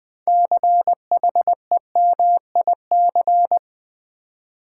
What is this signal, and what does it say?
Morse code, heard as a keyed tone, about 21 wpm, 700 Hz. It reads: CHEMIC
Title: Build A Word - 2 Letters At A Time - 8-Letter Words - 20wpm